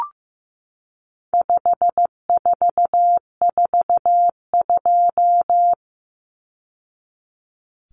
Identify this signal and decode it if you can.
Morse code, heard as a keyed tone, about 15 wpm, 700 Hz. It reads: 5442
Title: Numbers - 4-Digit 15wpm